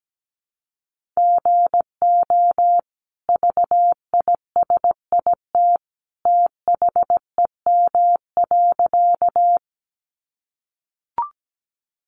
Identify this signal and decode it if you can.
Morse code, heard as a keyed tone, about 17 wpm, 700 Hz. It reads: GO VISIT THEM.
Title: Sentences from Top 900 Words 17wpm